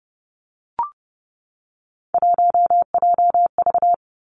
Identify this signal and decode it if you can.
Morse code, heard as a keyed tone, about 30 wpm, 700 Hz. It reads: 1JV
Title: Three Challenging Letter-Number Combinations 30wpm